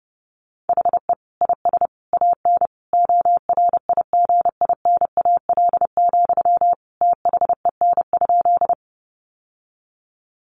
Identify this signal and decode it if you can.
Morse code, heard as a keyed tone, about 30 wpm, 700 Hz. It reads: HE IS AN ORIGINAL, THEN?
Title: Sentences from Top 900 Words 30wpm